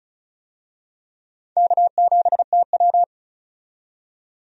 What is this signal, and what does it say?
Morse code, heard as a keyed tone, about 36 wpm, 700 Hz. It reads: K7TW
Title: USA Call Signs - 4 Characters - 35wpm